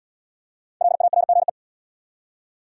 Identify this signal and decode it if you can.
Morse code, heard as a keyed tone, about 65 wpm, 700 Hz. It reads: LARGE